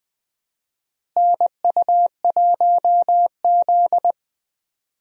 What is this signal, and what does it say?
Morse code, heard as a keyed tone, about 20 wpm, 700 Hz. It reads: NU1Z
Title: USA Call Signs - 4 Characters - Repeated 2x - 20wpm